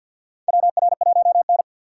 Morse code, heard as a keyed tone, about 53 wpm, 700 Hz. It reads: WR1N
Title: USA Call Signs - 4x Character Spacing - 50wpm